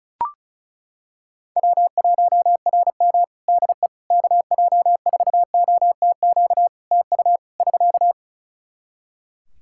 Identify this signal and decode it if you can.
Morse code, heard as a keyed tone, about 36 wpm, 700 Hz. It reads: W1RM DE KJ4OTQ TU <SK>
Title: Practice QSO - Exchange 4 - Ending - 35wpm